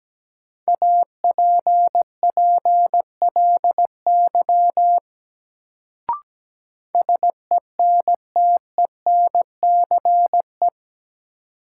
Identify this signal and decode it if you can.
Morse code, heard as a keyed tone, about 17 wpm, 700 Hz. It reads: APPLY SENTENCE
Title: Sentences from Top 1200 Words - New Word Drill Alt - 17wpm